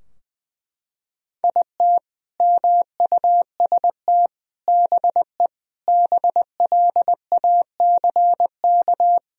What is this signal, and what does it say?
Morse code, heard as a keyed tone, about 20 wpm, 700 Hz. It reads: IT MUST BE BLACK
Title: Sentences from Top 1000 Words - Review Part 2 - 20wpm